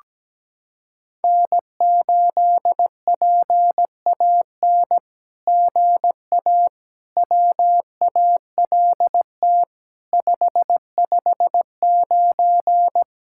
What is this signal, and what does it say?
Morse code, heard as a keyed tone, about 17 wpm, 700 Hz. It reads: N8PAN GA WALT 559
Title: SOTA QSO Practice 17wpm